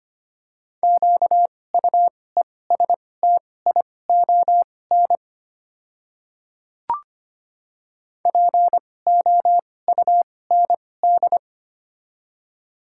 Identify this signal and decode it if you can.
Morse code, heard as a keyed tone, about 25 wpm, spaced as for 18 wpm, 700 Hz. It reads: QUESTION POUND
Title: 2x Character Spacing - Top 500 Words - 25wpm